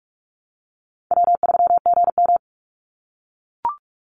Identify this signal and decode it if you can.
Morse code, heard as a keyed tone, about 44 wpm, 700 Hz. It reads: W3GM
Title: USA Call Signs 45wpm